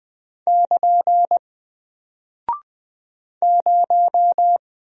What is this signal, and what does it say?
Morse code, heard as a keyed tone, about 20 wpm, 700 Hz. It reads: <KN> 0
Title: Single Character (Letter:Number:Punctuation:Prosign) Alt 20wpm